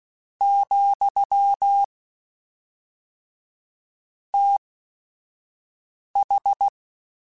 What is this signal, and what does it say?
Morse code, heard as a keyed tone, about 16 wpm, spaced as for 5 wpm, 800 Hz. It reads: , TH
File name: Technical Subjects / Technical Subjects-mp3s / XL @5